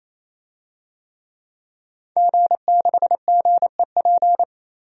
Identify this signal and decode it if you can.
Morse code, heard as a keyed tone, about 28 wpm, 700 Hz. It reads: G6GEP
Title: DXCC-100 Call Signs - 5 Characters - 28wpm